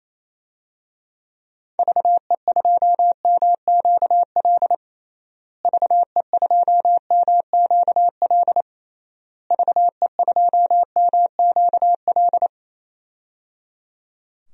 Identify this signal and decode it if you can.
Morse code, heard as a keyed tone, about 28 wpm, 700 Hz. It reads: VE2MQL VE2MQL VE2MQL
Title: DXCC-100 Call Signs - 6 Characters - Repeated 3x - 28wpm